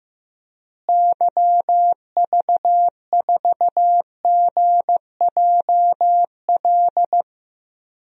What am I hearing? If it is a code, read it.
Morse code, heard as a keyed tone, about 15 wpm, 700 Hz. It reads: YV4GJL
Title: DXCC-100 Call Signs - 6 Characters - Repeated 2x - 15wpm